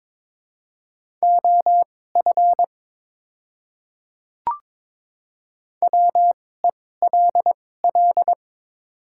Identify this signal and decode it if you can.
Morse code, heard as a keyed tone, about 21 wpm, spaced as for 16 wpm, 700 Hz. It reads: OF WELL
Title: 2x Character Spacing - Top 300 Words - 22wpm